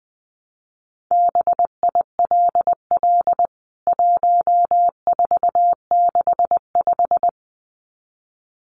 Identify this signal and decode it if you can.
Morse code, heard as a keyed tone, about 20 wpm, 700 Hz. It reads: BILL 1465
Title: CWT Exchange 20wpm